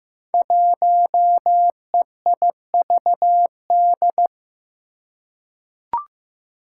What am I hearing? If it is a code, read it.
Morse code, heard as a keyed tone, about 15 wpm, 700 Hz. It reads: 1EIVD
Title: Five Challenging Letter-Number Combinations 15wpm